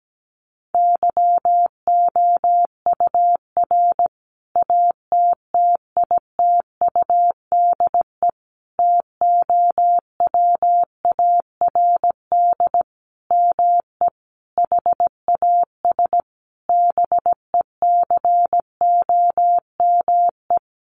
Morse code, heard as a keyed tone, about 17 wpm, 700 Hz. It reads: YOUR ATTITUDE TOWARD ME HAS BECOME
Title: Sentences from Top 1300 Words 17wpm